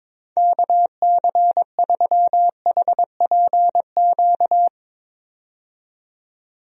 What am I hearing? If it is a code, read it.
Morse code, heard as a keyed tone, about 22 wpm, 700 Hz. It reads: KC3HPQ
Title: USA Call Signs - 6 Characters - 22wpm